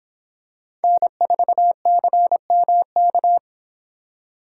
Morse code, heard as a keyed tone, about 26 wpm, 700 Hz. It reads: N4CMK